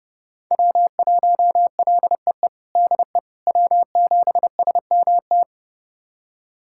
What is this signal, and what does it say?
Morse code, heard as a keyed tone, about 30 wpm, 700 Hz. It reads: W1LEE DE W7SMT
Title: USA Callsign DE Callsign - 5 Characters - 30wpm